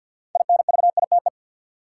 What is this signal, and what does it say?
Morse code, heard as a keyed tone, about 50 wpm, 700 Hz. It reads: INVITE
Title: Six Letter Words - 50wpm